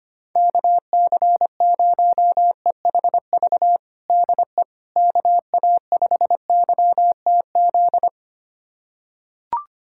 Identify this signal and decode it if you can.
Morse code, heard as a keyed tone, about 25 wpm, 700 Hz. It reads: KC0EHV DE KA5YTZ
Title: USA Callsign DE Callsign - 6 Characters - 25wpm